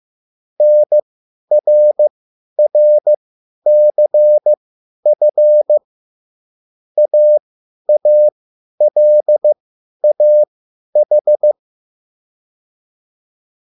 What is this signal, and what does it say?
Morse code, heard as a keyed tone, about 15 wpm, spaced as for 10 wpm, 600 Hz. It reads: NRRCF AALAH
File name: QSL / CW lesson